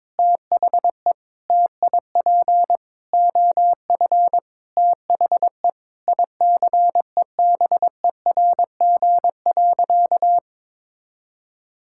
Morse code, heard as a keyed tone, about 22 wpm, 700 Hz. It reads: THE TIP OF THE ICEBERG.
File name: Sentences with 7-Word Phrases - 22wpm